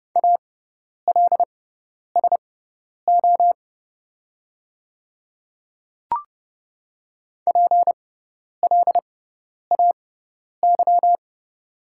Morse code, heard as a keyed tone, about 31 wpm, spaced as for 10 wpm, 700 Hz. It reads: ALSO PLAY
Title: 6x Character Spacing - Top 300 Words - 30wpm